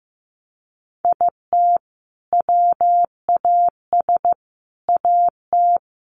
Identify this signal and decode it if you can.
Morse code, heard as a keyed tone, about 15 wpm, 700 Hz. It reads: IT WAS AT
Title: Sentences from Top 1500 Words 15wpm